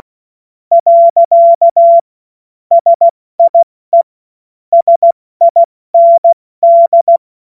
Morse code, heard as a keyed tone, about 16 wpm, 680 Hz. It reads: . SIE SIND
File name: Morsekurs / Witze / Witz 28. Dezember 2021, 70 F80